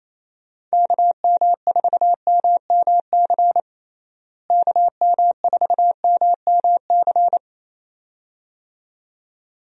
Morse code, heard as a keyed tone, about 28 wpm, 700 Hz. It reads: KM4MMC KM4MMC
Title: USA Call Signs - 6 Characters - Repeated 2x - 28wpm